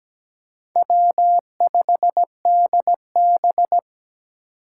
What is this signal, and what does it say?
Morse code, heard as a keyed tone, about 17 wpm, 700 Hz. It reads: W5DB
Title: USA Call Signs - 4 Characters - 4x Character Spacing - 17wpm